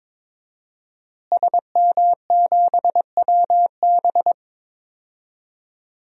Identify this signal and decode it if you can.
Morse code, heard as a keyed tone, about 22 wpm, 700 Hz. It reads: SM7WB
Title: DXCC-100 Call Signs - 5 Characters - Repeated 2x - 22wpm